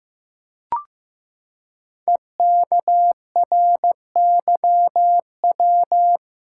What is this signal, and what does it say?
Morse code, heard as a keyed tone, about 15 wpm, 700 Hz. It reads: EKRYW
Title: Code Groups - 5 Letters - 15wpm